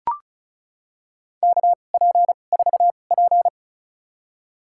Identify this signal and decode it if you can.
Morse code, heard as a keyed tone, about 36 wpm, spaced as for 25 wpm, 700 Hz. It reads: KP4P
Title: USA Call Signs - 2x Character Spacing - 35wpm